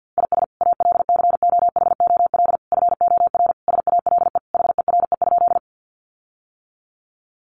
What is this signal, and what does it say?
Morse code, heard as a keyed tone, about 50 wpm, 700 Hz. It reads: IS ALCOHOL FOR SALE HERE?
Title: Sentences from Top 1400 Words 50wpm